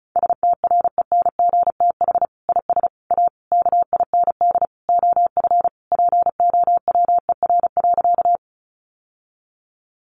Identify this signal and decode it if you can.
Morse code, heard as a keyed tone, about 34 wpm, 700 Hz. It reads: STRENGTH IS A KIND OF POWER.
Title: Sentences from Top 1300 Words 35wpm